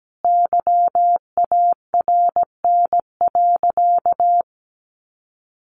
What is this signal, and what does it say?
Morse code, heard as a keyed tone, about 17 wpm, 700 Hz. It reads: YARN.
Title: Additional Easy Phrases 17wpm